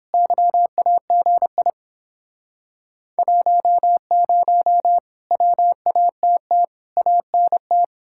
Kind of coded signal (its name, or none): Morse code